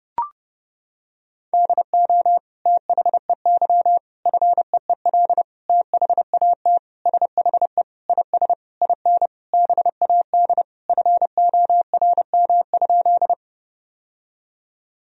Morse code, heard as a keyed tone, about 31 wpm, 700 Hz. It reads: DO THEY FEEL THAT SHE IS IN BAD FORM?
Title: Sentences from Top 1000 Words - Review Part 1 - 30wpm